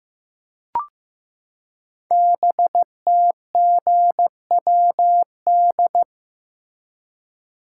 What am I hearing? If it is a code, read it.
Morse code, heard as a keyed tone, about 15 wpm, 700 Hz. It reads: BTGWD
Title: Code Groups - 5 Letters - 15wpm